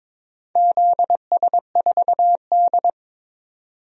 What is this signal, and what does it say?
Morse code, heard as a keyed tone, about 22 wpm, 700 Hz. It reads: ZS4D
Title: DXCC-100 Call Signs - 4 Characters - 22wpm